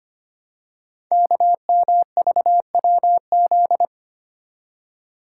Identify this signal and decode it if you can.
Morse code, heard as a keyed tone, about 25 wpm, 700 Hz. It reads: KMVWZ
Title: Code Groups - 5 Letters - 25wpm